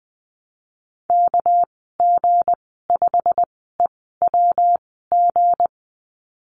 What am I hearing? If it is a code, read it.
Morse code, heard as a keyed tone, about 20 wpm, spaced as for 15 wpm, 700 Hz. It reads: KG5EWG
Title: 200 Lesson 32g - Letter k - Call Signs 20wpm